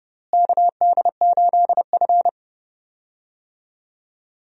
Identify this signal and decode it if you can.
Morse code, heard as a keyed tone, about 32 wpm, 700 Hz. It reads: KD8F
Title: USA Call Signs - 4 Characters - Repeated 2x - 30wpm